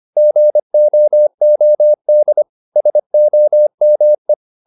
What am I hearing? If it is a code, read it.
Morse code, heard as a keyed tone, about 25 wpm, 600 Hz. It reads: GOOD SOME
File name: Top-100-Words-25